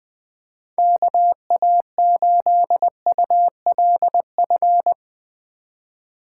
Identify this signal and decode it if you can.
Morse code, heard as a keyed tone, about 20 wpm, 700 Hz. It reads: KA8ULF
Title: USA Call Signs - 6 Characters - 3x Character Spacing - 20wpm